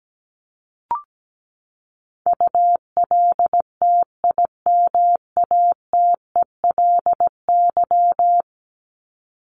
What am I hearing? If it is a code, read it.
Morse code, heard as a keyed tone, about 17 wpm, 700 Hz. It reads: ULTIMATELY
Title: LY Suffix - Top 100 Words with Base Words 17wpm